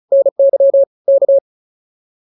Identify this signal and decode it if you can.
Morse code, heard as a keyed tone, about 34 wpm, 550 Hz. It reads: NY K
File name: POTA210 35